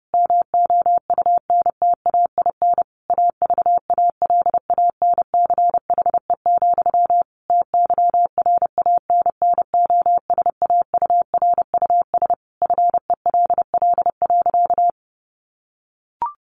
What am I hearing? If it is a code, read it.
Morse code, heard as a keyed tone, about 28 wpm, 700 Hz. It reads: MOUNTAIN AVALANCHE, TYRANNOSAURUS FELL.
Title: A Sound Of Thunder 30wpm